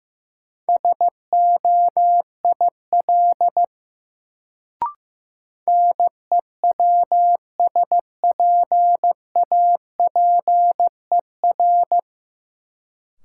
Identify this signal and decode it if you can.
Morse code, heard as a keyed tone, about 15 wpm, 700 Hz. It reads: SOIL NEWSPAPER
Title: Sentences from Top 1000 Words - Review Part 4 - New Word Drill - 15wpm